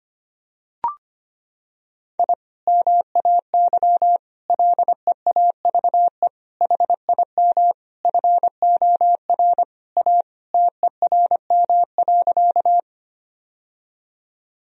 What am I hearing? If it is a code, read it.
Morse code, heard as a keyed tone, about 25 wpm, 700 Hz. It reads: I MAY LEAVE HIM FOR A TERM.
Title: Sentences from Top 1000 Words - Review Part 4 - 25wpm